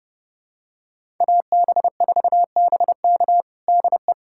Morse code, heard as a keyed tone, about 32 wpm, 700 Hz. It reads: AB4BK DE
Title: USA Callsign DE Callsign - 5 Characters - 30wpm